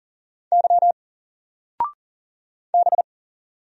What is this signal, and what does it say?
Morse code, heard as a keyed tone, about 39 wpm, 700 Hz. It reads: Y B
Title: Single Character (Letter:Number:Punctuation:Prosign) Alt 40wpm